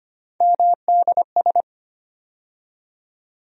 Morse code, heard as a keyed tone, about 25 wpm, 700 Hz. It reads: MDS
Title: USA Call Sign Postfixes - 3 Letters 25wpm